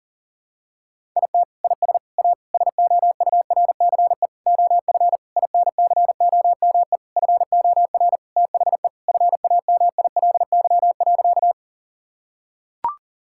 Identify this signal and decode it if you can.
Morse code, heard as a keyed tone, about 37 wpm, 700 Hz. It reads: IT IS A SOURCE OF INCOME FOR THE FAMILY.
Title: Sentences from Top 1000 Words - Review Part 5 - 40wpm